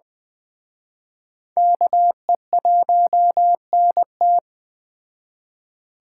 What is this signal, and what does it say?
Morse code, heard as a keyed tone, about 21 wpm, 700 Hz. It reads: KE1NT
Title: Speed Racing - USA 2x2 Call Signs 17wpm